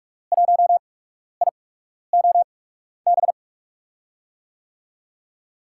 Morse code, heard as a keyed tone, about 44 wpm, spaced as for 12 wpm, 700 Hz. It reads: 1IOB
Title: USA Call Signs - 6 Characters - 8x Character Spacing - 45wpm